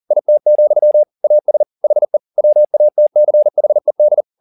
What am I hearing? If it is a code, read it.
Morse code, heard as a keyed tone, about 40 wpm, 600 Hz. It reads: IT, AS HE WATCHED
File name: the old man and the sea 2 40wpm